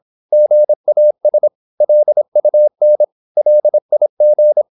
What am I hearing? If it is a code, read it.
Morse code, heard as a keyed tone, about 26 wpm, 600 Hz. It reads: GAS LUN LIG
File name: Korte ord DK 26wpm